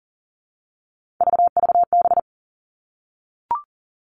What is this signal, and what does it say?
Morse code, heard as a keyed tone, about 41 wpm, 700 Hz. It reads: VVB
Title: Three Challenging Letter-Number Combinations 40wpm